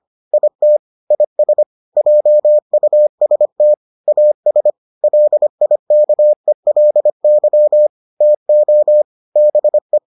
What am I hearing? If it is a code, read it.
Morse code, heard as a keyed tone, about 25 wpm, 600 Hz. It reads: IT IS JUST AS LIKELY TO BE